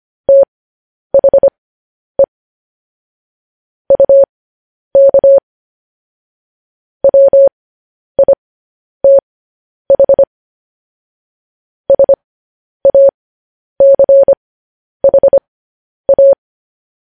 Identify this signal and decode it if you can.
Morse code, heard as a keyed tone, about 25 wpm, spaced as for 10 wpm, 550 Hz. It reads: THE UK WITH SACHA